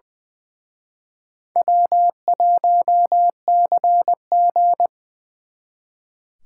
Morse code, heard as a keyed tone, about 24 wpm, 700 Hz. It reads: W1CG
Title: Speed Racing - USA Call Signs 20wpm